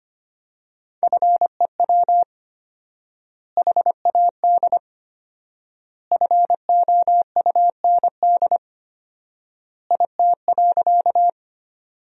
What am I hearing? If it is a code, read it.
Morse code, heard as a keyed tone, about 25 wpm, 700 Hz. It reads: FEW HAD FOUND IT.
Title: 4x Word-Spacing - Sentences from Top 300 Words - 25wpm